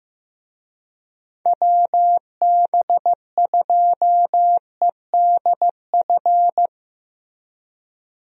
Morse code, heard as a keyed tone, about 15 wpm, 700 Hz. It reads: WB2EDF
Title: USA Call Signs - 6 Characters - Repeated 2x - 15wpm